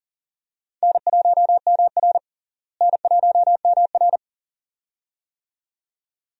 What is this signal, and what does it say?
Morse code, heard as a keyed tone, about 40 wpm, 700 Hz. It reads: N1MR N1MR
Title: USA Call Signs - 4 Characters - Repeated 2x - 40wpm